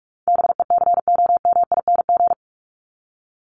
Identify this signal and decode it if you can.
Morse code, heard as a keyed tone, about 44 wpm, 700 Hz. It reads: BECOMING
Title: ING Suffix - Top 100 Words with Base Words 45wpm